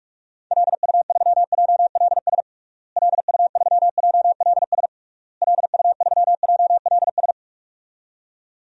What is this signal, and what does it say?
Morse code, heard as a keyed tone, about 46 wpm, 700 Hz. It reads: LU3JLS LU3JLS LU3JLS
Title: DXCC-100 Call Signs - 6 Characters - Repeated 3x - 45wpm